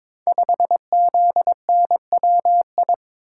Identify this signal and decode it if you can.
Morse code, heard as a keyed tone, about 22 wpm, 700 Hz. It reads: 5ZNWI
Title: Code Groups - 5 Letter:Numbers - 22wpm